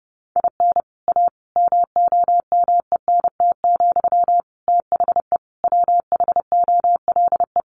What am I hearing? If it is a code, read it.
Morse code, heard as a keyed tone, about 30 wpm, 700 Hz. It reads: IN A MOMENT, THE WHOLE
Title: Sentences from Top 1000 Words 30wpm